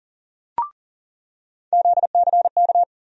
Morse code, heard as a keyed tone, about 37 wpm, 700 Hz. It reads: ZCK
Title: USA Call Sign Postfixes - 3 Letters 40wpm